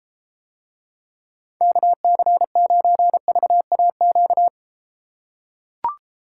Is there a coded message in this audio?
Yes: Morse code from 1.6 to 4.5 s, heard as a keyed tone, about 33 wpm, 700 Hz. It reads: KC9VAQ